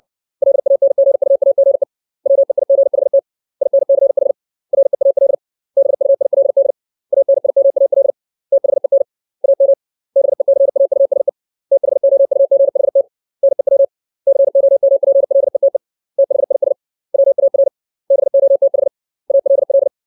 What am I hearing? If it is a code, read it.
Morse code, heard as a keyed tone, about 60 wpm, 550 Hz. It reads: LANGUAGE WEIGHT INCH REAL BUILD ANIMAL THEN AM BECAUSE THOUGHT NEW COMPLETE THIS WAR BOTH ALL
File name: 632 words / 632 Words 60WPM